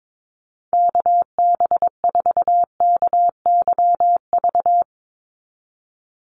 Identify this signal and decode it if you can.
Morse code, heard as a keyed tone, about 22 wpm, 700 Hz. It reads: KB4KYV
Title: USA Call Signs 22wpm